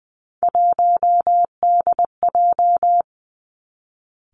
Morse code, heard as a keyed tone, about 19 wpm, 700 Hz. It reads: 1DJ
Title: Three Challenging Letter-Number Combinations 20wpm